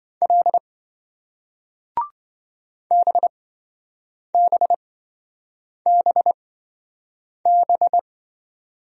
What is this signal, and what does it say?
Morse code, heard as a keyed tone, about 28 wpm, spaced as for 7 wpm, 700 Hz. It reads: L BBBB